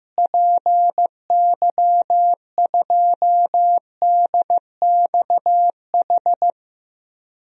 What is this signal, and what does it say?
Morse code, heard as a keyed tone, about 15 wpm, 700 Hz. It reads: PY2DXH
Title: DXCC-100 Call Signs - 6 Characters - Repeated 3x - 15wpm